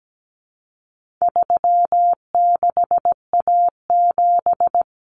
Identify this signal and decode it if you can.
Morse code, heard as a keyed tone, about 17 wpm, 700 Hz. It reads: 36A7
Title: Common Cut Numbers - 4-Digit 17wpm